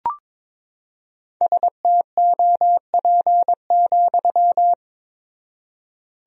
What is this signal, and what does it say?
Morse code, heard as a keyed tone, about 22 wpm, 700 Hz. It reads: STOP,
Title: 4x Word-Spacing - Sentences from Top 300 Words - 22wpm